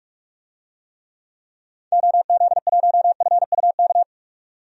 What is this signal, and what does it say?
Morse code, heard as a keyed tone, about 46 wpm, 700 Hz. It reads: OZ1FUK
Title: DXCC-100 Call Signs - 6 Characters - 45wpm